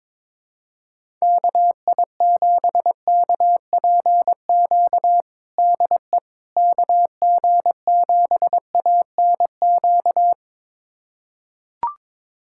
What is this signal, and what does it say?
Morse code, heard as a keyed tone, about 21 wpm, 700 Hz. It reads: KI7KPQ DE KG7ANQ
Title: USA Callsign DE Callsign - 6 Characters - 22wpm